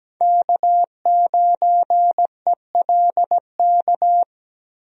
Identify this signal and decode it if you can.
Morse code, heard as a keyed tone, about 17 wpm, 700 Hz. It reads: K9ELK
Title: USA Call Signs - 5 Characters - Repeated 2x - 17wpm